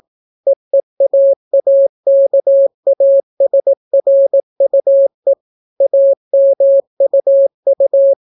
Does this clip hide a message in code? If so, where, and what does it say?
Yes: Morse code from 0.5 to 8.1 s, heard as a keyed tone, about 18 wpm, 550 Hz. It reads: EEAAKASRUE AMUU